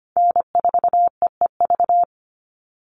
Morse code, heard as a keyed tone, about 25 wpm, 700 Hz. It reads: N4EEV
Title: Call Signs Repeated 3x - 25wpm